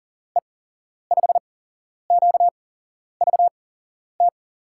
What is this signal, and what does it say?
Morse code, heard as a keyed tone, about 40 wpm, spaced as for 11 wpm, 700 Hz. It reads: E5QVT
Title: USA Call Signs - 6 Characters - 8x Character Spacing - 40wpm